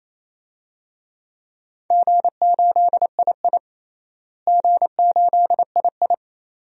Morse code, heard as a keyed tone, about 28 wpm, 700 Hz. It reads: G8II G8II
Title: DXCC-100 Call Signs - 4 Characters - Repeated 2x - 28wpm